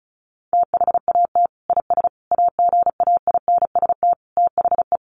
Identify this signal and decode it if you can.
Morse code, heard as a keyed tone, about 34 wpm, 700 Hz. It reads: THAT IS AGAINST THE
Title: Sentences from Top 1500 Words 35wpm